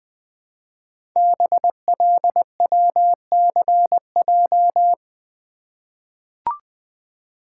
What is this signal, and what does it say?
Morse code, heard as a keyed tone, about 20 wpm, 700 Hz. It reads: BLWCJ
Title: Code Groups - 5 Letters - 20wpm